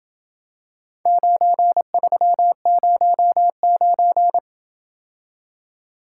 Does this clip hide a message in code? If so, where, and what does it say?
Yes: Morse code from 1.1 to 4.4 s, heard as a keyed tone, about 26 wpm, 700 Hz. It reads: 9309